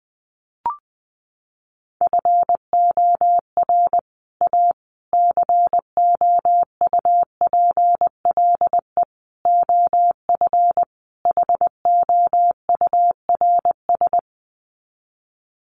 Morse code, heard as a keyed tone, about 20 wpm, 700 Hz. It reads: FOR A COUPLE OF HOURS
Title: Sets of 5 Words Encore2 20wpm